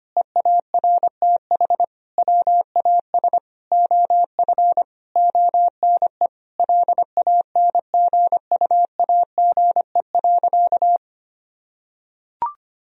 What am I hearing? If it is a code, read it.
Morse code, heard as a keyed tone, about 25 wpm, 700 Hz. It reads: EARTH WAS OF ONE LANGUAGE.
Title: Sentences from Top 1000 Words - Review Part 3 - 25wpm